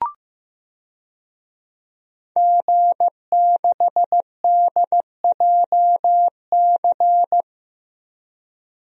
Morse code, heard as a keyed tone, about 16 wpm, 700 Hz. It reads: G6DJC
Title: DXCC-100 Call Signs - 5 Characters - 15wpm